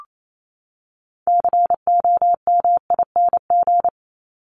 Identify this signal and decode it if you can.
Morse code, heard as a keyed tone, about 28 wpm, 700 Hz. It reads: COMING
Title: ING Suffix - Top 100 Words with Base Words 28wpm